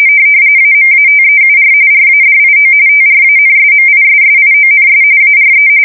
RTTY, about 45.45 baud. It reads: AMATEUR RADIO TRANSMISSIONS ARE
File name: fsk170 / teletype-RTTY